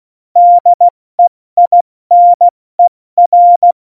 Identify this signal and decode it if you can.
Morse code, heard as a keyed tone, about 16 wpm, 700 Hz. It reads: DEINER
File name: Morsekurs / Witze / Witz 17.Juli 2018, 70 F80